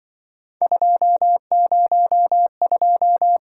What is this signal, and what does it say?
Morse code, heard as a keyed tone, about 24 wpm, 700 Hz. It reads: 202
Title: Speed-Racing - Three Letter-Number Combinations - 20wpm